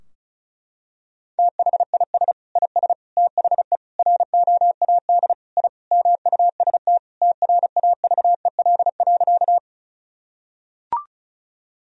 Morse code, heard as a keyed tone, about 36 wpm, 700 Hz. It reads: THIS IS THE ROAD I MUST TRAVEL.
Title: Sentences from Top 1000 Words - Review Part 2 - 35wpm